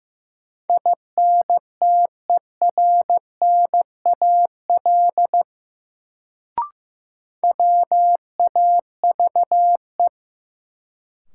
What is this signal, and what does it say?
Morse code, heard as a keyed tone, about 15 wpm, 700 Hz. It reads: INTERNAL WAVE
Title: Sentences from Top 1400 Words - New Word Drill Alt - 15wpm